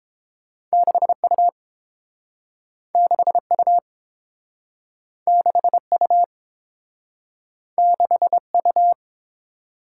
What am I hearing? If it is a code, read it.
Morse code, heard as a keyed tone, about 32 wpm, 700 Hz. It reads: 6U 6U 6U 6U